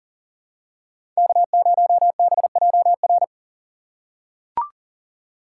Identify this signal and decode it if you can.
Morse code, heard as a keyed tone, about 40 wpm, 700 Hz. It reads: K0BJR
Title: USA Call Signs - 5 Characters - 6x Character Spacing - 40wpm